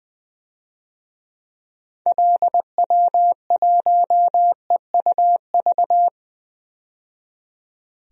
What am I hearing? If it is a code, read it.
Morse code, heard as a keyed tone, about 21 wpm, 700 Hz. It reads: LW1EUV
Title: DXCC-100 Call Signs - 6 Characters - 20wpm